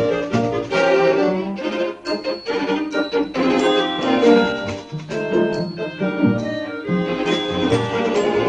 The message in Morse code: none